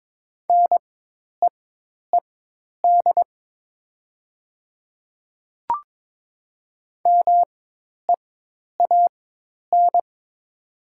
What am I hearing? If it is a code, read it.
Morse code, heard as a keyed tone, about 22 wpm, spaced as for 10 wpm, 700 Hz. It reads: NEED MEAN
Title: 4x Character Spacing - Top 200 Words - 22wpm